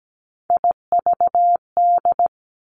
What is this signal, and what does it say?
Morse code, heard as a keyed tone, about 17 wpm, 700 Hz. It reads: IVD
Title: Three Challenging Letter-Number Combinations 17wpm